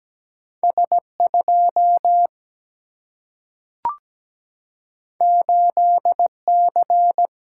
Morse code, heard as a keyed tone, about 17 wpm, 700 Hz. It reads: S2 8C
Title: Single-Letter Single-Number 17wpm